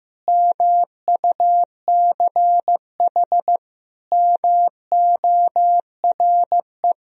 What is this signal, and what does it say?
Morse code, heard as a keyed tone, about 16 wpm, 700 Hz. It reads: MUCH MORE
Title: Sentences from Top 1000 Words - Review Part 1 - 15wpm